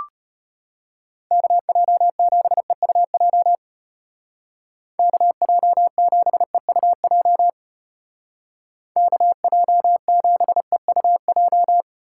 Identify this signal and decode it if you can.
Morse code, heard as a keyed tone, about 32 wpm, 700 Hz. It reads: KJ7EUJ KJ7EUJ KJ7EUJ